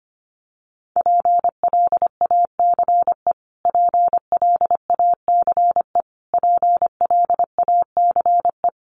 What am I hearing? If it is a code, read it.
Morse code, heard as a keyed tone, about 25 wpm, 700 Hz. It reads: PLACE PLACE PLACE
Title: Top 200 Words Repeated 25wpm